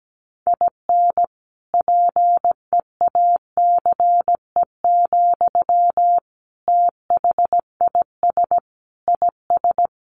Morse code, heard as a keyed tone, about 17 wpm, 700 Hz. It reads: IN PEACE, THIS IS
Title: Sentences from Top 1500 Words 17wpm